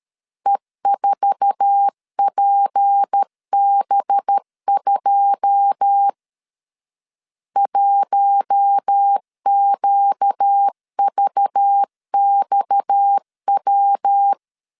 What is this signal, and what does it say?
Morse code, heard as a keyed tone, about 13 wpm, 800 Hz. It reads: E4PB2 1QVXW
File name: CW / DET / 1 9 1500C S B 12wpm 3819